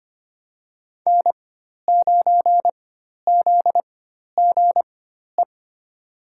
Morse code, heard as a keyed tone, about 25 wpm, spaced as for 12 wpm, 700 Hz. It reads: N9ZGE
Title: USA Call Signs - 5 Characters - 4x Character Spacing - 25wpm